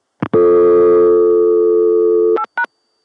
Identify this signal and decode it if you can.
Dial tone (North America), 350 + 440 Hz.